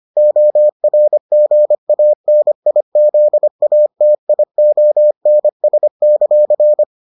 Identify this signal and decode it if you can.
Morse code, heard as a keyed tone, about 25 wpm, 600 Hz. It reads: ORGANIZATIONS;